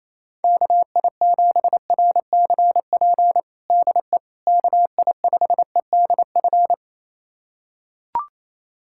Morse code, heard as a keyed tone, about 28 wpm, 700 Hz. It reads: KI7RCP DE KI5EDF
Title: USA Callsign DE Callsign 28wpm